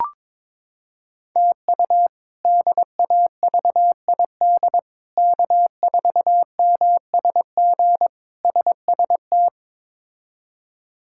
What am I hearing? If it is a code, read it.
Morse code, heard as a keyed tone, about 21 wpm, 700 Hz. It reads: TU DAVID K4MSG SST
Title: K1USN SST QSO Practice - 22wpm